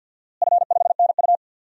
Morse code, heard as a keyed tone, about 49 wpm, 700 Hz. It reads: F5NU
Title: DXCC-100 Call Signs - 4 Characters - Repeated 3x - 50wpm